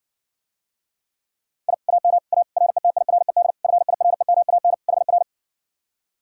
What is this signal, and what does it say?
Morse code, heard as a keyed tone, about 60 wpm, 700 Hz. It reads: I AM A RETIRED FIREMAN HR